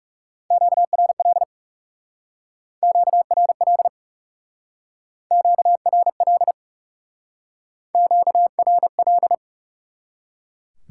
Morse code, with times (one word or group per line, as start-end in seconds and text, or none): 0.5-1.4 QRL
2.8-3.9 QRL
5.3-6.5 QRL
8.0-9.3 QRL